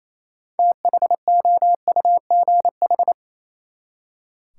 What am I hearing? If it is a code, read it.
Morse code, heard as a keyed tone, about 28 wpm, 700 Hz. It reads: THOUGH
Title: Six Letter Words - 28wpm